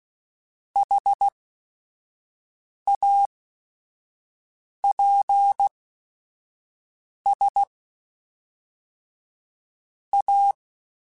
Morse code, heard as a keyed tone, about 16 wpm, spaced as for 5 wpm, 800 Hz. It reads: HAPS A
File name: Technical Subjects / Technical Subjects-mp3s / Sw Pwr Sup @5